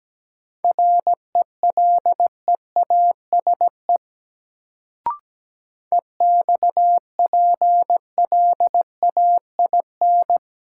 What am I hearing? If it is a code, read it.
Morse code, heard as a keyed tone, about 17 wpm, 700 Hz. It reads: RELEASE EXPLAIN
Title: Sentences from Top 1100 Words - New Word Drill - 17wpm